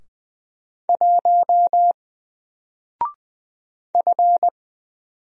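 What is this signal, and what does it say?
Morse code, heard as a keyed tone, about 20 wpm, 700 Hz. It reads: 1 F
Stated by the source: Single letter and number Morse drill at 20 wpm, from N4LCD.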